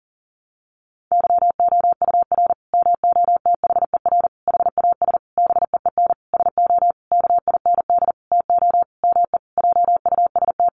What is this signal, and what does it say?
Morse code, heard as a keyed tone, about 40 wpm, 700 Hz. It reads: YOUR MOTHER HAS BEEN SO KIND TO ME JUST
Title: Sentences from Top 1200 Words 40wpm